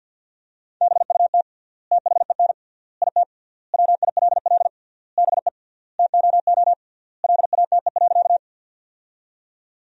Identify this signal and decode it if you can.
Morse code, heard as a keyed tone, about 50 wpm, 700 Hz. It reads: BUT THEN IT WILL BE TOO LATE.
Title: 3x Word-Spacing - Sentences from Top 300 Words - 50wpm